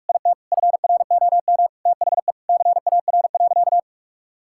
Morse code, heard as a keyed tone, about 41 wpm, 700 Hz. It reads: IT FROM THE CAR.
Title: Sentences from Top 1000 Words - Review Part 5 - 45wpm